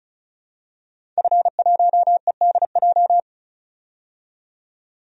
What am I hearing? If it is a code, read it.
Morse code, heard as a keyed tone, about 34 wpm, 700 Hz. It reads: F1EDJ
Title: DXCC-100 Call Signs - 5 Characters - 35wpm